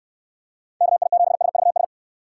Morse code, heard as a keyed tone, about 68 wpm, 700 Hz. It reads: KI7SVS